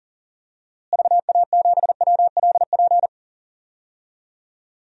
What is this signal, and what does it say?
Morse code, heard as a keyed tone, about 40 wpm, 700 Hz. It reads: VA7WLP
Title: DXCC-100 Call Signs - 6 Characters - Repeated 2x - 40wpm